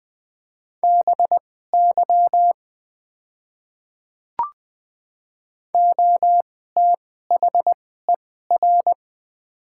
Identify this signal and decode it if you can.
Morse code, heard as a keyed tone, about 20 wpm, spaced as for 14 wpm, 700 Hz. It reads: BY OTHER
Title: 2x Character Spacing - Top 300 Words - 20wpm